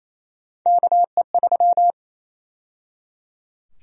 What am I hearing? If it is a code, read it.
Morse code, heard as a keyed tone, about 27 wpm, 700 Hz. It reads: KE3
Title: USA Call Sign Prefixes - 28wpm